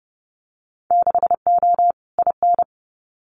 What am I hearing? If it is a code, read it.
Morse code, heard as a keyed tone, about 29 wpm, 700 Hz. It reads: 6O IN
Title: Winter Field Day Exchanges Encore 30wpm